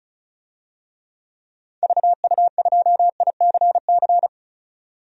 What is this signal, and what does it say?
Morse code, heard as a keyed tone, about 36 wpm, 700 Hz. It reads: VU2ICC
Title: DXCC-100 Call Signs - 6 Characters - 35wpm